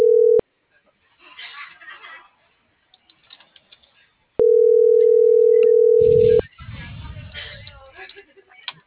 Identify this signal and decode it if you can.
Ringback tone (North America), 440 + 480 Hz.